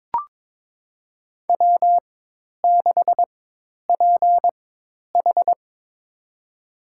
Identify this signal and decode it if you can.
Morse code, heard as a keyed tone, about 22 wpm, spaced as for 10 wpm, 700 Hz. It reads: W6PH
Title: USA Call Signs - 4 Characters - 4x Character Spacing - 22wpm